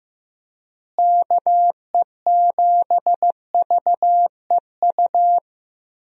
Morse code, heard as a keyed tone, about 15 wpm, 700 Hz. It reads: KE7VEU
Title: USA Call Signs - 6 Characters - Repeated 2x - 15wpm